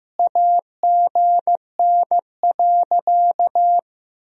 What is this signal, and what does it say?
Morse code, heard as a keyed tone, about 15 wpm, 700 Hz. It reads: AGN.
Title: Sentences from Top 300 Words and Abbreviations 15wpm